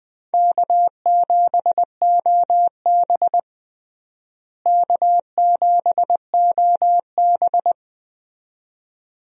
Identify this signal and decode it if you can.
Morse code, heard as a keyed tone, about 20 wpm, 700 Hz. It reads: K7OB K7OB
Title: USA Call Signs - 4 Characters - Repeated 2x - 20wpm